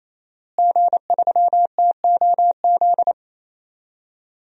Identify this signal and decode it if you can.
Morse code, heard as a keyed tone, about 28 wpm, 700 Hz. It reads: G3TOZ
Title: DXCC-100 Call Signs - 5 Characters - Repeated 3x - 28wpm